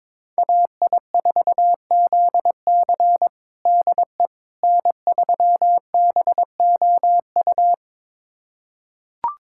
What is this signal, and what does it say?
Morse code, heard as a keyed tone, about 22 wpm, 700 Hz. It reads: AI4ZC DE N3BOU
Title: USA Callsign DE Callsign - 5 Characters - 22wpm